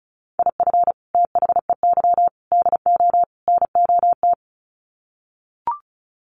Morse code, heard as a keyed tone, about 34 wpm, 700 Hz. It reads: IF THEY DO NOT
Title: Sets of 4 Words 35wpm